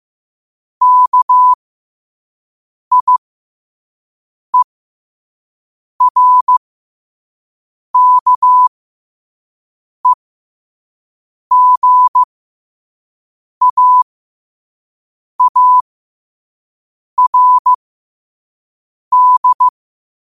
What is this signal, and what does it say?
Morse code, heard as a keyed tone, about 15 wpm, spaced as for 5 wpm, 1000 Hz. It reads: KIERKEGAARD
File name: qotdfiles / 5wpm